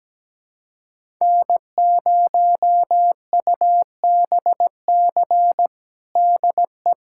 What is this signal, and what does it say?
Morse code, heard as a keyed tone, about 17 wpm, 700 Hz. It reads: N0UBC DE
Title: USA Callsign DE Callsign - 5 Characters - 17wpm